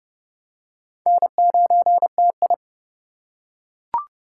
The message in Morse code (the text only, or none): N9TI